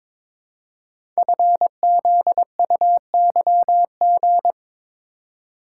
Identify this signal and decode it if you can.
Morse code, heard as a keyed tone, about 22 wpm, 700 Hz. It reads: FZUYG
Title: Code Groups - 5 Letter:Numbers - 22wpm